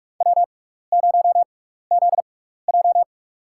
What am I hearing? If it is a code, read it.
Morse code, heard as a keyed tone, about 43 wpm, spaced as for 15 wpm, 700 Hz. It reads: W0ZJ
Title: USA Call Signs - 4 Characters - 6x Character Spacing - 45wpm